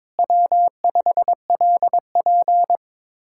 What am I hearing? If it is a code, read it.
Morse code, heard as a keyed tone, about 22 wpm, 700 Hz. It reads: W5LP
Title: Speed-Racing Common QSO Phrases - 22wpm